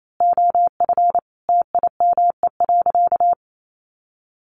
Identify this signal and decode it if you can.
Morse code, heard as a keyed tone, about 28 wpm, 700 Hz. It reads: OF TIME.
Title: A Sound Of Thunder 28wpm